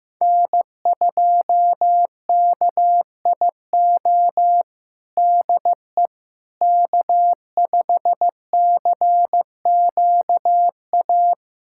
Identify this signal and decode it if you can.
Morse code, heard as a keyed tone, about 16 wpm, 700 Hz. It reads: N2KIO DE K5CQA
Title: USA Callsign DE Callsign - 5 Characters - 15wpm